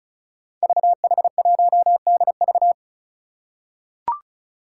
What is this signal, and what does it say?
Morse code, heard as a keyed tone, about 36 wpm, 700 Hz. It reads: VH1DV
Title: Five Challenging Letter-Number Combinations 35wpm